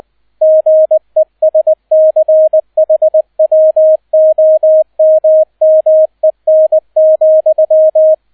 Morse code, heard as a keyed tone, about 20 wpm, 630 Hz, with radio noise. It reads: GESCHWOMMEN,